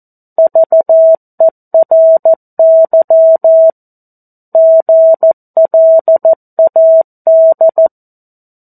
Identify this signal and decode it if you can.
Morse code, heard as a keyed tone, about 14 wpm, 650 Hz. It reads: VERY GLAD
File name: 02-Chapter-02 14 WPM